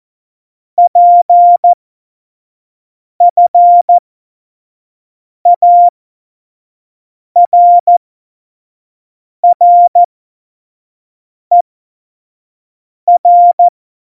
Morse code, heard as a keyed tone, about 14 wpm, spaced as for 5 wpm, 700 Hz. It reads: PFARRER
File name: Morsekurs / Witze / Witz 29. Dezember 2020, 25 F70